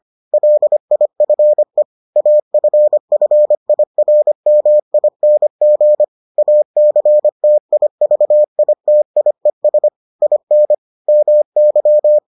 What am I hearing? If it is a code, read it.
Morse code, heard as a keyed tone, about 25 wpm, 600 Hz. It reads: LIFE AFFIRMING ACTIVITIES IN MY